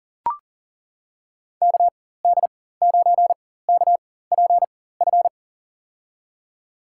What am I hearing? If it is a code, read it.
Morse code, heard as a keyed tone, about 40 wpm, spaced as for 19 wpm, 700 Hz. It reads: KD9KPF
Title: USA Call Signs - 6 Characters - 4x Character Spacing - 40wpm